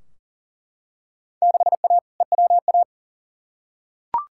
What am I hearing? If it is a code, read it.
Morse code, heard as a keyed tone, about 40 wpm, 700 Hz. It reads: 6A EWA
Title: Field Day Exchanges Encore - 40wpm